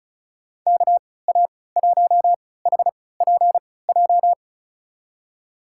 Morse code, heard as a keyed tone, about 36 wpm, spaced as for 20 wpm, 700 Hz. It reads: KA1HPJ
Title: USA Call Signs - 6 Characters - 3x Character Spacing - 35wpm